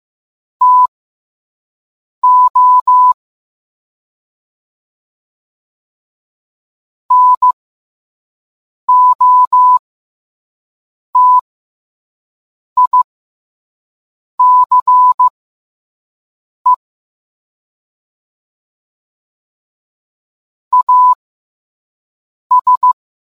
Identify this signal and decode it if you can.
Morse code, heard as a keyed tone, about 15 wpm, spaced as for 5 wpm, 1000 Hz. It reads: TO NOTICE AS